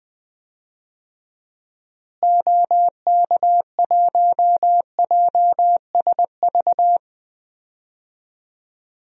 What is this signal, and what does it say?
Morse code, heard as a keyed tone, about 20 wpm, 700 Hz. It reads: OK1JSV
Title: DXCC-100 Call Signs - 6 Characters - Repeated 2x - 20wpm